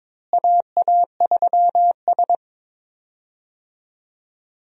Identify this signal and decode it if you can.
Morse code, heard as a keyed tone, about 22 wpm, 700 Hz. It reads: AA3S
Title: USA Call Signs - 4 Characters - Repeated 2x - 22wpm